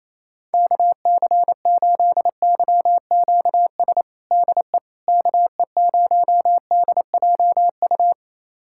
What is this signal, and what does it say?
Morse code, heard as a keyed tone, about 28 wpm, 700 Hz. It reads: KC8YQS DE KE0DJU
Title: USA Callsign DE Callsign 28wpm